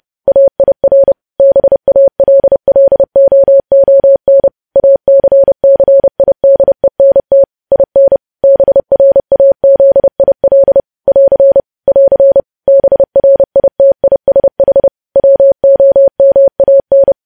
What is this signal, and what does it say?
Morse code, heard as a keyed tone, about 30 wpm, 550 Hz. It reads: AIR BALLOON ACCIDENT IN BRAZIL <AR> <AR> BRITISH WOMAN
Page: news headlines in morse code at 30 wpm